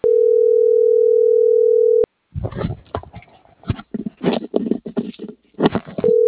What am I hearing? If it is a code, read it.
Ringback tone (North America), 440 + 480 Hz.